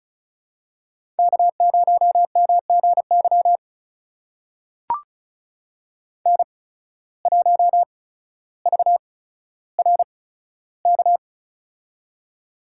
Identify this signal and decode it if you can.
Morse code, heard as a keyed tone, about 36 wpm, spaced as for 15 wpm, 700 Hz. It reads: K0MGY N1VRK
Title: USA Call Signs - 5 Characters - 8x Character Spacing - 35wpm